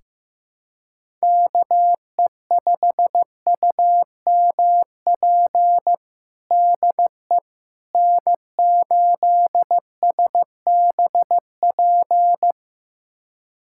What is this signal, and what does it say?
Morse code, heard as a keyed tone, about 15 wpm, 700 Hz. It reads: KE5UMP DE N8SBP
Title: USA Callsign DE Callsign 15wpm